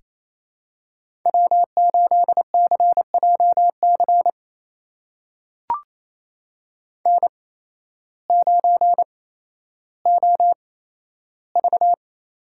Morse code, heard as a keyed tone, about 28 wpm, spaced as for 13 wpm, 700 Hz. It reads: W8CJC N9OV
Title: USA Call Signs - 5 Characters - 8x Character Spacing - 28wpm